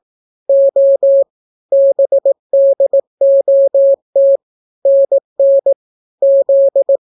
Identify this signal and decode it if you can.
Morse code, heard as a keyed tone, about 18 wpm, 550 Hz. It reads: O BDOT NN Z